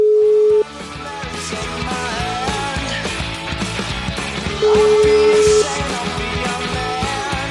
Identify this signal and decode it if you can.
Ringback tone (Europe, 425 Hz), with noise.